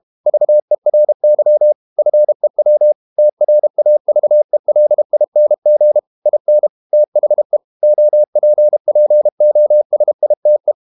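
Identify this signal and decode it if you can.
Morse code, heard as a keyed tone, about 32 wpm, 600 Hz. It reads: VERY FEW TRAVELING IN THE OPPOSITE
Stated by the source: Morse code practice at 32 wpm from CW med Gnister, reading Ken Follett's novel Never in English.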